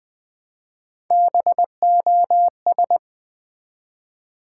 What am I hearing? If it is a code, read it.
Morse code, heard as a keyed tone, about 20 wpm, 700 Hz. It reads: BOS
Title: Build A Word - 1 Letter At A Time - 4-Letter Words - 20wpm